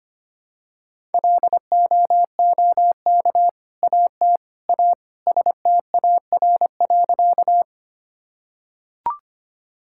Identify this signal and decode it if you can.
Morse code, heard as a keyed tone, about 25 wpm, 700 Hz. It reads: LOOK AT A STAR.
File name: practice / 25wpm / Sentences with 2-Word Phrases - 25wpm